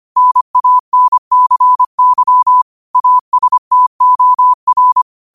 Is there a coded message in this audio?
Yes: Morse code from 0.2 to 5.0 s, heard as a keyed tone, about 25 wpm, 1000 Hz. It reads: NANCY ASTOR